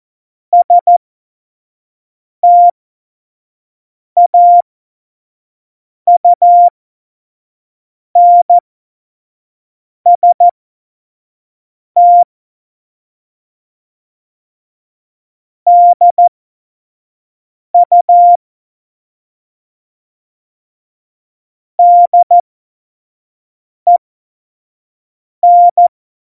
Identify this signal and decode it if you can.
Morse code, heard as a keyed tone, about 14 wpm, spaced as for 5 wpm, 700 Hz. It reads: STAUNST DU DEN
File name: Morsekurs / Witze / Witz 31.Juli 2018, 25 F70